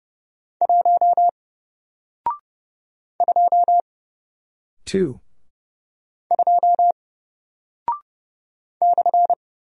Morse code, heard as a keyed tone, about 30 wpm, 700 Hz. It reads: 1 2 2 /
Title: Single Character (Letter:Number:Punctuation:Prosign) Alt 30wpm